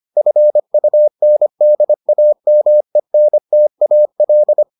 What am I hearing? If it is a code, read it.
Morse code, heard as a keyed tone, about 25 wpm, 600 Hz. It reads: FUNDAMENTAL